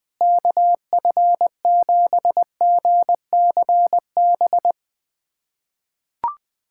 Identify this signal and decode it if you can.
Morse code, heard as a keyed tone, about 20 wpm, 700 Hz. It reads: KF7GCB
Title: USA Callsign DE Callsign - 6 Characters - 20wpm